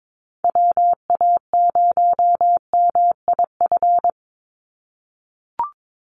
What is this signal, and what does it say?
Morse code, heard as a keyed tone, about 22 wpm, 700 Hz. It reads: WA0MIF
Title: USA Call Signs 22wpm